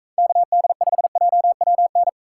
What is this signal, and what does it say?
Morse code, heard as a keyed tone, about 39 wpm, 700 Hz. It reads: KD5JWN